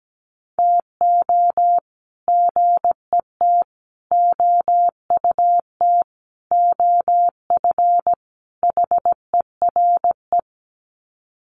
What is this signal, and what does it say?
Morse code, heard as a keyed tone, about 17 wpm, 700 Hz. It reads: TO GET OUT OF HERE
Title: Sets of 5 Words Encore 17wpm